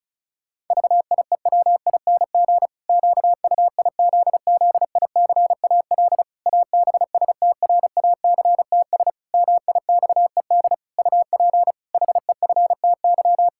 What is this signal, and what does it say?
Morse code, heard as a keyed tone, about 36 wpm, 700 Hz. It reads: VIEWING QUIZZICAL ABSTRACTS MIXED UP HEFTY
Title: Pangram Challenge v2 - 35wpm